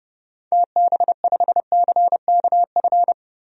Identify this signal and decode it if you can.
Morse code, heard as a keyed tone, about 30 wpm, 700 Hz. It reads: TB5CKF
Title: DXCC-100 Call Signs - 6 Characters - Repeated 2x - 30wpm